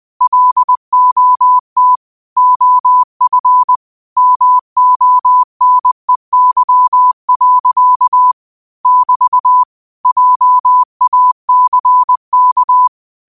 Morse code, heard as a keyed tone, about 20 wpm, 1000 Hz. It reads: LOT OF MONEY. <BT> JACK
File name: qotdfiles / 20wpm